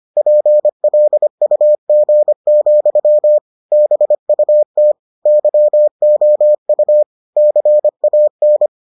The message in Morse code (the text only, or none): PLUG, BUT YOU CAN